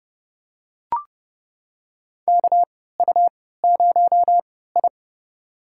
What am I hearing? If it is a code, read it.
Morse code, heard as a keyed tone, about 30 wpm, spaced as for 17 wpm, 700 Hz. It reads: KU0I
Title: USA Call Signs - 3x Character Spacing - 30wpm